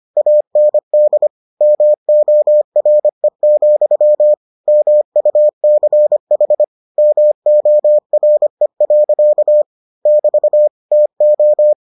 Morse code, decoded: AND MORE, MUCH MORE. <BT> TO

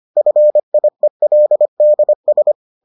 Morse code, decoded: FIELDS

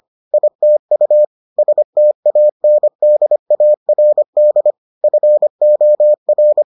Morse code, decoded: ITU STANDARD FOR

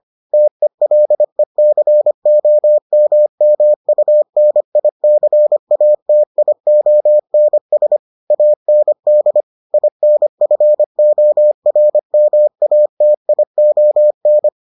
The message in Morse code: TELECOMMUNICATIONS AND INFORMATION

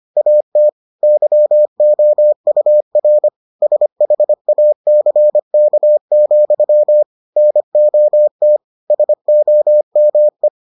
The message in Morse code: AT YOUR SHACK, NOT SOME

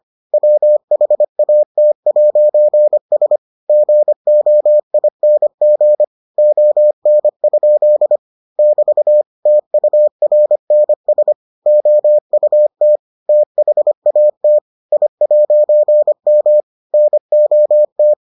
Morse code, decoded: WHAT'S GOING ON? <BT> TURNS OUT THAT I'M NOT

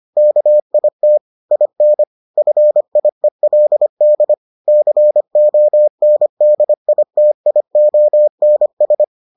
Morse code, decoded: KIT IN FIELD CONDITIONS

